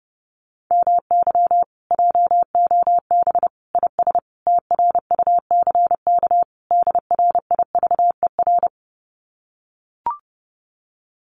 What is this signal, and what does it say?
Morse code, heard as a keyed tone, about 30 wpm, 700 Hz. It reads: MY JOB IS TRUCK DRIVER
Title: Common QSO Phrases Encore 30wpm